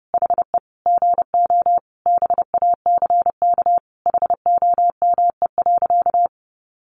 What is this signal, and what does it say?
Morse code, heard as a keyed tone, about 30 wpm, 700 Hz. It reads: HE GO BACK HOME.
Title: Sentences from Top 500 Words 30wpm